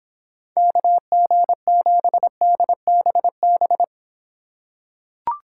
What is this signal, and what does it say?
Morse code, heard as a keyed tone, about 26 wpm, 700 Hz. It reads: KG7DBB